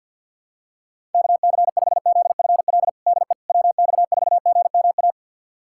Morse code, heard as a keyed tone, about 47 wpm, 700 Hz. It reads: KC5ZFL DE WX4GMA
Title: USA Callsign DE Callsign - 6 Characters - 50wpm